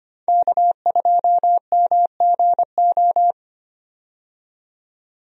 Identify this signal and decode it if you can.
Morse code, heard as a keyed tone, about 25 wpm, 700 Hz. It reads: K2MGO
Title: USA Call Signs - 5 Characters - 25wpm